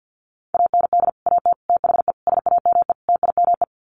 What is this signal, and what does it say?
Morse code, heard as a keyed tone, about 50 wpm, 700 Hz. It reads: AND AT THE SAME TIME
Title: Sets of 5 Words 50wpm